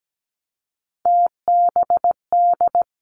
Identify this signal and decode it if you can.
Morse code, heard as a keyed tone, about 17 wpm, 700 Hz. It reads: TBD
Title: Three Challenging Letter-Number Combinations 17wpm